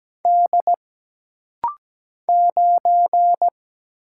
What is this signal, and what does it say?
Morse code, heard as a keyed tone, about 17 wpm, 700 Hz. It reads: D 9
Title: Single Letter-Number Alt - Rapid-Fire 17wpm